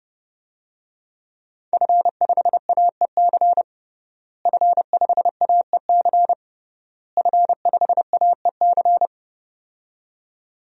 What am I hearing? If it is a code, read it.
Morse code, heard as a keyed tone, about 32 wpm, 700 Hz. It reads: F5AEC F5AEC F5AEC
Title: DXCC-100 Call Signs - 5 Characters - Repeated 3x - 30wpm